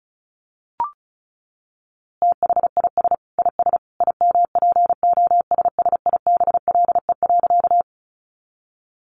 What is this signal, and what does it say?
Morse code, heard as a keyed tone, about 34 wpm, 700 Hz. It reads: THIS IS IMPOSSIBLE.
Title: A Sound Of Thunder 35wpm